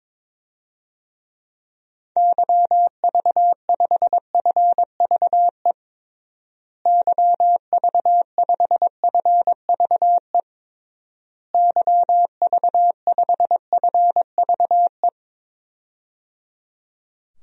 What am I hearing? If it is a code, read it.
Morse code, heard as a keyed tone, about 22 wpm, 700 Hz. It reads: YV5FVE YV5FVE YV5FVE
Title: DXCC-100 Call Signs - 6 Characters - Repeated 3x - 22wpm